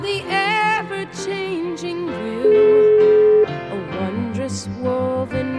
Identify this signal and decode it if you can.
Ringback tone (Europe, 425 Hz), with noise.